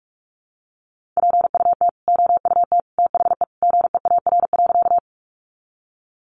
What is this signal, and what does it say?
Morse code, heard as a keyed tone, about 44 wpm, 700 Hz. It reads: PUT OUT THE GEAR.
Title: Sentences from Top 1400 Words 45wpm